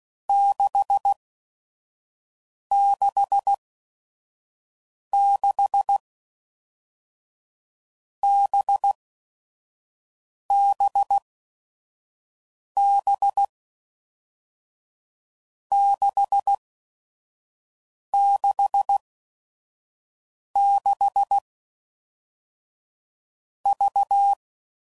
Morse code, heard as a keyed tone, about 16 wpm, spaced as for 5 wpm, 800 Hz. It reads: 666 BBB 666 V
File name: Groups / Basic Letter Groups / VB46